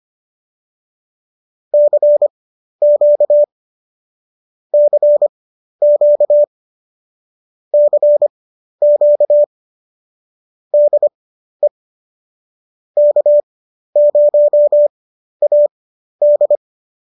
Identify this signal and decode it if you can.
Morse code, heard as a keyed tone, about 25 wpm, spaced as for 12 wpm, 590 Hz. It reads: CQ CQ CQ DE K0AD
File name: qso008 12